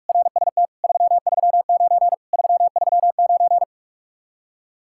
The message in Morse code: RST 339 339

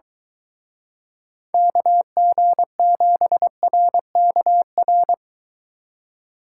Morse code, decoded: KG7RKR

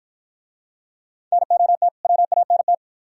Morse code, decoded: NOT WANT

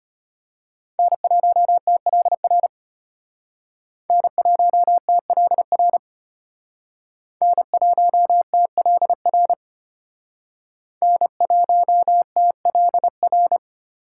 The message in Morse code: N1TLR N1TLR N1TLR N1TLR

